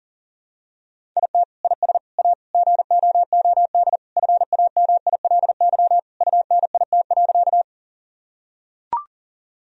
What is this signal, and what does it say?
Morse code, heard as a keyed tone, about 40 wpm, 700 Hz. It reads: IT IS A GOOD FAMILY UNIT.
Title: Sentences from Top 1000 Words - Review Part 2 - 40wpm